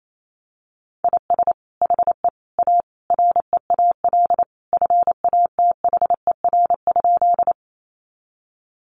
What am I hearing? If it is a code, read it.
Morse code, heard as a keyed tone, about 28 wpm, 700 Hz. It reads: IS HE A REAL FATHER?
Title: Sentences from Top 300 Words 28wpm